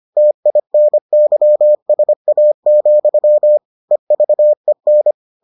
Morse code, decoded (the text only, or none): TINYSA, EVEN